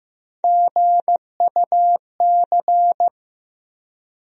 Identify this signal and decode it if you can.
Morse code, heard as a keyed tone, about 15 wpm, 700 Hz. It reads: GUC
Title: USA Call Sign Postfixes - 3 Letters 15wpm